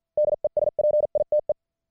Morse code, heard as a keyed tone, about 50 wpm, spaced as for 44 wpm, 600 Hz. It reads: DESPITE
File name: archived / CWBros